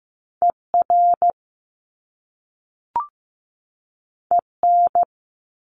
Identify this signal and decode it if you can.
Morse code, heard as a keyed tone, about 15 wpm, 700 Hz. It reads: ER EN
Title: Top Two Letter Combinations in Words 15wpm